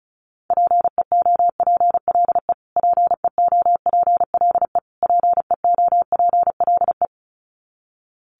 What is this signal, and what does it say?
Morse code, heard as a keyed tone, about 34 wpm, 700 Hz. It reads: PEOPLE PEOPLE PEOPLE
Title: Top 500 Words Repeated 35wpm